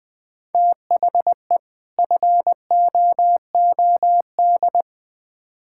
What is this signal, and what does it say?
Morse code, heard as a keyed tone, about 21 wpm, 700 Hz. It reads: THE FOOD
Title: Build A Phrase - 5-Word Phrases - 20wpm